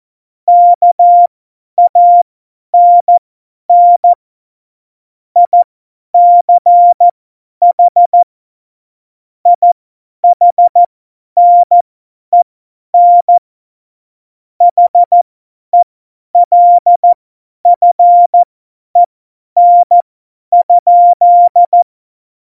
Morse code, heard as a keyed tone, about 14 wpm, spaced as for 10 wpm, 700 Hz. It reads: KANN ICH IHNEN HELFEN?